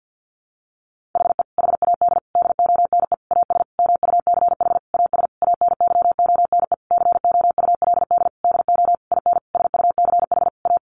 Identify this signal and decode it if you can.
Morse code, heard as a keyed tone, about 50 wpm, 700 Hz. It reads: HE HAD DONE AS MUCH AS ANYONE COULD DO IN SUCH A
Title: Sentences from Top 1200 Words 50wpm